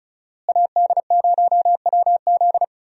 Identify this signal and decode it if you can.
Morse code, heard as a keyed tone, about 34 wpm, 700 Hz. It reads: AD0WZ
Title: USA Call Signs - 5 Characters - Repeated 2x - 35wpm